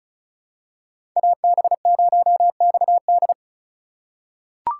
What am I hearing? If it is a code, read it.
Morse code, heard as a keyed tone, about 36 wpm, 700 Hz. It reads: AB0XD
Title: USA Call Signs - 2x Character Spacing - 35wpm